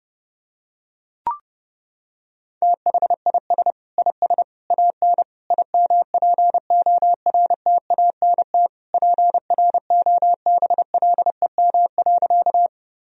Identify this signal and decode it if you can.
Morse code, heard as a keyed tone, about 32 wpm, 700 Hz. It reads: THIS IS AN IMPORTANT PROBLEM.
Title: Sentences from Top 1000 Words - Review Part 2 - 30wpm